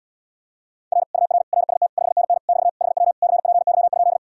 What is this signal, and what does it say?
Morse code, heard as a keyed tone, about 79 wpm, 700 Hz. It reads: R UR RST 5NN <BT> HW CPY?